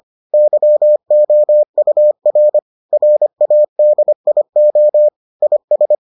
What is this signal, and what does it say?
Morse code, heard as a keyed tone, about 25 wpm, 600 Hz. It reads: YOUR RADIO IS